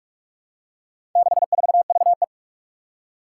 Morse code, heard as a keyed tone, about 41 wpm, 700 Hz. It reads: 64VE
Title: Four Challenging Letter-Number Combinations 45wpm